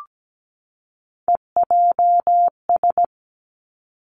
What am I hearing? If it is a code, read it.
Morse code, heard as a keyed tone, about 17 wpm, 700 Hz. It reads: EJS
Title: Three Challenging Letter-Number Combinations 17wpm